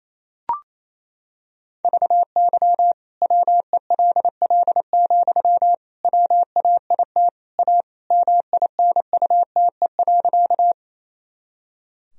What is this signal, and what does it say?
Morse code, heard as a keyed tone, about 28 wpm, 700 Hz. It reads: VY WELL, WAIT A MINUTE.
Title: Sentences from Top 500 Words and Abbreviations 28wpm